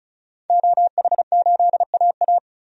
Morse code, heard as a keyed tone, about 34 wpm, 700 Hz. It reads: OH8AA
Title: DXCC-100 Call Signs - 5 Characters - 35wpm